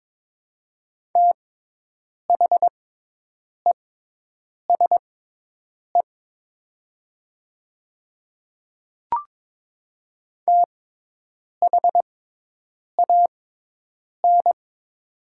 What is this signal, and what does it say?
Morse code, heard as a keyed tone, about 22 wpm, spaced as for 8 wpm, 700 Hz. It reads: THESE THAN
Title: 6x Character Spacing - Top 200 Words - 22wpm